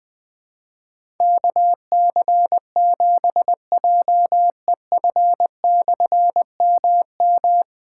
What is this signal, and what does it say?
Morse code, heard as a keyed tone, about 20 wpm, 700 Hz. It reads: KC7JEF/MM
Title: Difficult Call Signs 20wpm